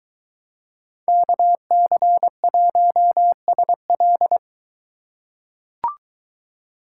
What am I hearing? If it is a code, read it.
Morse code, heard as a keyed tone, about 23 wpm, 700 Hz. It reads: KC1SL